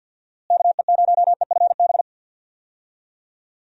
Morse code, heard as a keyed tone, about 50 wpm, 700 Hz. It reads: KE0EFB
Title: USA Call Signs - 6 Characters - Repeated 3x - 50wpm